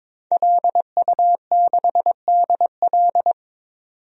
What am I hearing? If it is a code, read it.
Morse code, heard as a keyed tone, about 21 wpm, 700 Hz. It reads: LU6DL
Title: DXCC-100 Call Signs - 5 Characters - 22wpm